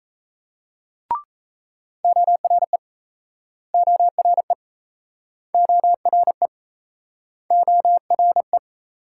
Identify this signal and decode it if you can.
Morse code, heard as a keyed tone, about 40 wpm, 700 Hz. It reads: ORE ORE ORE ORE